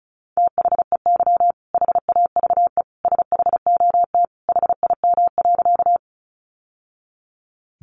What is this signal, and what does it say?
Morse code, heard as a keyed tone, about 34 wpm, 700 Hz. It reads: THEY HAVE SHOT HIM.
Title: Sentences from Top 1100 Words 35wpm